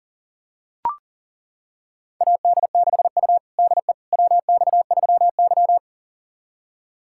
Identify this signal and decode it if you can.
Morse code, heard as a keyed tone, about 40 wpm, 700 Hz. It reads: AD6U DE WX3Y
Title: USA Callsign DE Callsign - 4 Characters - 40wpm